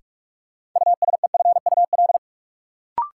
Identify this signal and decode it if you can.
Morse code, heard as a keyed tone, about 46 wpm, 700 Hz. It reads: USEFUL
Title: Sentences from Top 700 Words - New Word Drill - 45wpm